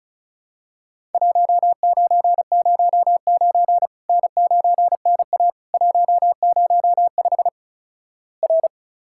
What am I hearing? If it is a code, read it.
Morse code, heard as a keyed tone, about 36 wpm, 700 Hz. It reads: 1909 N9NA 105 R
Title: Worked All Europe DX Contest - QTC Exchange - 35wpm